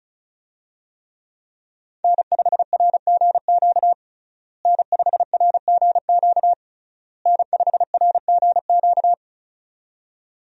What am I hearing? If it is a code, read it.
Morse code, heard as a keyed tone, about 36 wpm, 700 Hz. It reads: N5RGQ N5RGQ N5RGQ
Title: USA Call Signs - 5 Characters - Repeated 3x - 35wpm